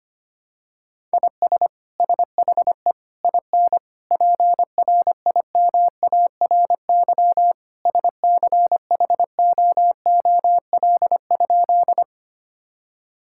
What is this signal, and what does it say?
Morse code, heard as a keyed tone, about 25 wpm, 700 Hz. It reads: IS SHE IN PRIMARY SCHOOL?
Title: Sentences from Top 1000 Words - Review Part 5 - 25wpm